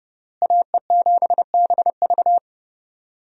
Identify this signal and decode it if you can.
Morse code, heard as a keyed tone, about 30 wpm, 700 Hz. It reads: AE7BV
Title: USA Call Signs - 5 Characters - Repeated 3x - 30wpm